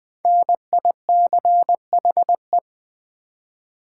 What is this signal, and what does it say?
Morse code, heard as a keyed tone, about 20 wpm, 700 Hz. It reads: NICHE
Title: 5 Letter Words - First 3 Letters Given - 20wpm